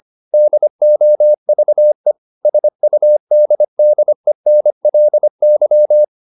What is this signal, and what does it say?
Morse code, heard as a keyed tone, about 25 wpm, 600 Hz. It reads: DOVE SUDDENLY
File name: the old man and the sea part 1 25wpm